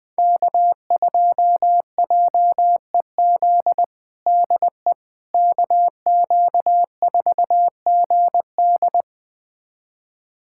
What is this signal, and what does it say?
Morse code, heard as a keyed tone, about 20 wpm, 700 Hz. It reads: K2JEZ DE KQ4GD
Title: USA Callsign DE Callsign - 5 Characters - 20wpm